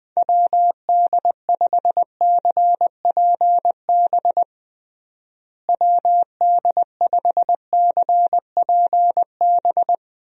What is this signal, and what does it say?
Morse code, heard as a keyed tone, about 20 wpm, 700 Hz. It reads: WD5CPB WD5CPB
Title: USA Call Signs - 6 Characters - Repeated 2x - 20wpm